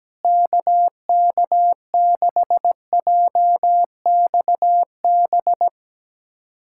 Morse code, heard as a keyed tone, about 17 wpm, 700 Hz. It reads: KK6JXB
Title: USA Callsign DE Callsign 17wpm